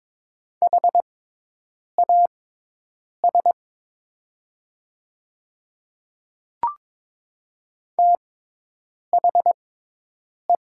Morse code, heard as a keyed tone, about 22 wpm, spaced as for 8 wpm, 700 Hz. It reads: HAS THE
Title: 6x Character Spacing - Top 100 Words - 22wpm